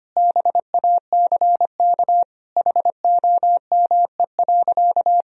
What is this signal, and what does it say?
Morse code, heard as a keyed tone, about 25 wpm, 700 Hz. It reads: BACK HOME.
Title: Sentences from Top 500 Words and Abbreviations 25wpm